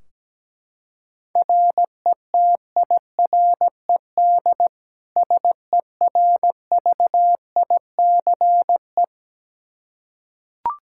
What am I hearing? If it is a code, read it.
Morse code, heard as a keyed tone, about 17 wpm, 700 Hz. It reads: RETIRED SERVICE
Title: K1USN SST QSO Practice 17wpm